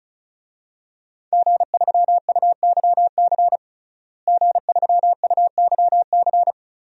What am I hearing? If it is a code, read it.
Morse code, heard as a keyed tone, about 36 wpm, 700 Hz. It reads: G3UYC G3UYC
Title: DXCC-100 Call Signs - 5 Characters - Repeated 3x - 35wpm